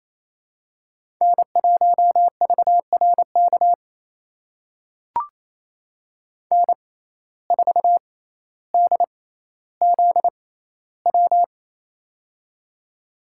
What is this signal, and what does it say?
Morse code, heard as a keyed tone, about 28 wpm, spaced as for 14 wpm, 700 Hz. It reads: N1VRK N4DZW
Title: USA Call Signs - 5 Characters - 6x Character Spacing - 28wpm